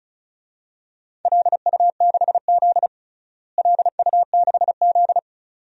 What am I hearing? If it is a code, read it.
Morse code, heard as a keyed tone, about 36 wpm, 700 Hz. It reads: LU6Z LU6Z
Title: DXCC-100 Call Signs - 4 Characters - Repeated 2x - 35wpm